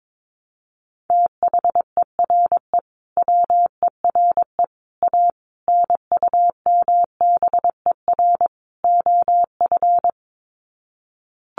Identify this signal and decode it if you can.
Morse code, heard as a keyed tone, about 22 wpm, 700 Hz. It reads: THERE WERE A NUMBER OF
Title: Sets of 5 Words Encore2 22wpm